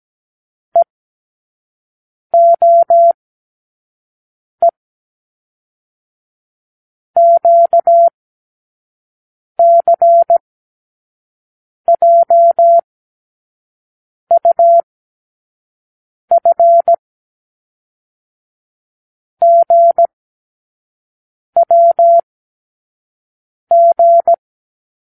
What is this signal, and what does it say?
Morse code, heard as a keyed tone, about 17 wpm, 680 Hz. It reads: E O E Q C J U F G W G
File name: k7qo / Lesson 36 (A-X)